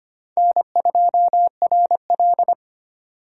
Morse code, heard as a keyed tone, about 32 wpm, 700 Hz. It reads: N2RL